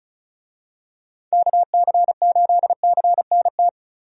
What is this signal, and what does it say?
Morse code, heard as a keyed tone, about 36 wpm, 700 Hz. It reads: KC8CNT
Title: USA Call Signs - 6 Characters - Repeated 2x - 35wpm